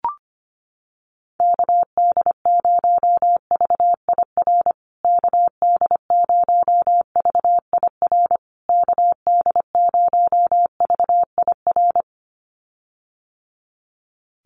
Morse code, heard as a keyed tone, about 25 wpm, 700 Hz. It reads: KD0VIR KD0VIR KD0VIR
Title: USA Call Signs Repeated 3x 25wpm